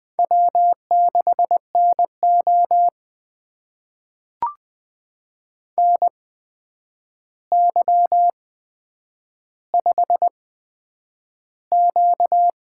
Morse code, heard as a keyed tone, about 20 wpm, spaced as for 9 wpm, 700 Hz. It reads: W6NO NY5Q
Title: USA Call Signs - 4 Characters - 8x Character Spacing - 20wpm